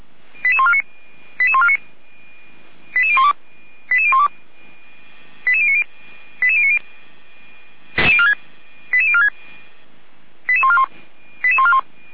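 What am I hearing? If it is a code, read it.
Five-tone selective call (ZVEI), heard as quick runs of five tones, tones of about 70 ms. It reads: 88149 88149 88012 88012 88090 88090 88046 88141 88141
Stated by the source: ZVEI five-tone selective call received by SDR, from Feuerwehr Orlamünde audio archive.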